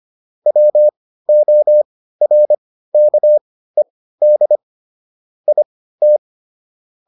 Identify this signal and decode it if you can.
Morse code, heard as a keyed tone, about 25 wpm, spaced as for 15 wpm, 600 Hz. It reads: WORKED IT